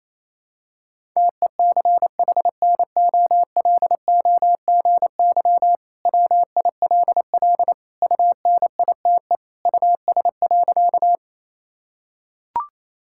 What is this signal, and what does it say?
Morse code, heard as a keyed tone, about 28 wpm, 700 Hz. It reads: TECHNOLOGY WILL UNITE US.
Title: Sentences from Top 1000 Words - Review Part 5 - 28wpm